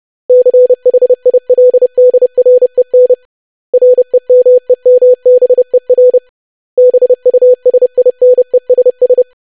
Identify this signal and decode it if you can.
Morse code, heard as a keyed tone, about 30 wpm, 500 Hz. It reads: CHILDREN REMEMBER BUSINESS
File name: lesson / web / basic500-25wpm